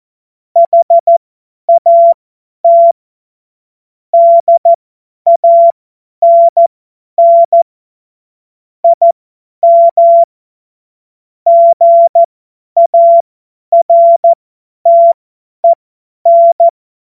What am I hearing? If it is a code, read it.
Morse code, heard as a keyed tone, about 14 wpm, spaced as for 10 wpm, 680 Hz. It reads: HAT DANN IM GARTEN
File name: Morsekurs / Witze / Witz 06. Juli 2021, 50 F70